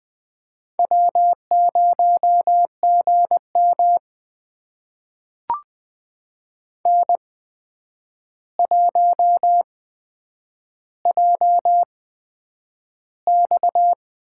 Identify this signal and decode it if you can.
Morse code, heard as a keyed tone, about 20 wpm, spaced as for 9 wpm, 700 Hz. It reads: W0GM N1JX
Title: USA Call Signs - 4 Characters - 8x Character Spacing - 20wpm